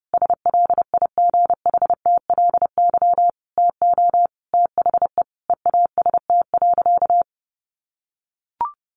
Morse code, heard as a keyed tone, about 30 wpm, 700 Hz. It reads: SLIGHTLY TO THE EAST.
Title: Sentences from Top 1300 Words 30wpm